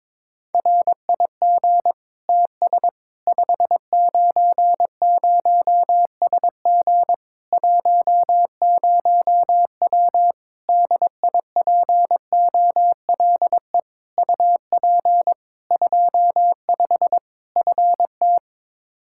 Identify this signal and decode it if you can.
Morse code, heard as a keyed tone, about 22 wpm, 700 Hz. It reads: RIG TS 590SG 10W DIPOLE UP 25 FT
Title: K1USN SST QSO Practice 22wpm